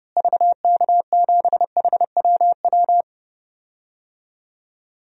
Morse code, heard as a keyed tone, about 30 wpm, 700 Hz. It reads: VK7HWW
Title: DXCC-100 Call Signs - 6 Characters - Repeated 3x - 30wpm